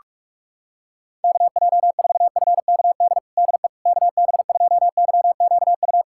Morse code, heard as a keyed tone, about 46 wpm, 700 Hz. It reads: KJ4FKD DE KB2YQU
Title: USA Callsign DE Callsign 45wpm